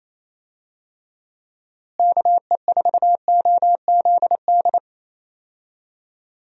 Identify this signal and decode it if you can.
Morse code, heard as a keyed tone, about 28 wpm, 700 Hz. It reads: KE4OZD
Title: USA Call Signs - 6 Characters - Repeated 3x - 28wpm